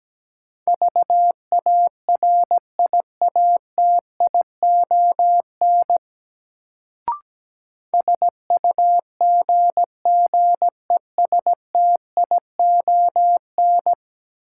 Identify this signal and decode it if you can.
Morse code, heard as a keyed tone, about 17 wpm, 700 Hz. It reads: VARIATION SUGGESTION
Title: Sentences from Top 1600 Words - New Word Drill - 17wpm